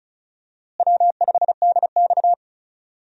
Morse code, heard as a keyed tone, about 34 wpm, 700 Hz. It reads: W5DX
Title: USA Call Signs - 4 Characters - Repeated 2x - 35wpm